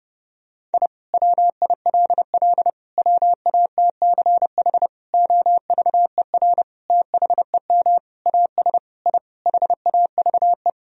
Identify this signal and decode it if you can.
Morse code, heard as a keyed tone, about 32 wpm, 700 Hz. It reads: I WILL WATCH OVER THEM AS I HAVE
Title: Sentences from Top 1000 Words - Review Part 1 - 30wpm